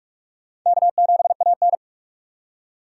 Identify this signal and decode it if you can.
Morse code, heard as a keyed tone, about 43 wpm, 700 Hz. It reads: K7AN
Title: USA Call Signs - 4 Characters - 2x Character Spacing - 45wpm